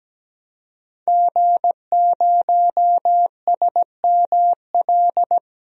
Morse code, heard as a keyed tone, about 17 wpm, 700 Hz. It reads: G0SML
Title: DXCC-100 Call Signs - 5 Characters - Repeated 2x - 17wpm